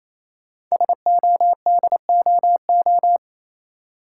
Morse code, heard as a keyed tone, about 28 wpm, 700 Hz. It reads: SODOO
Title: Code Groups - 5 Letters - 28wpm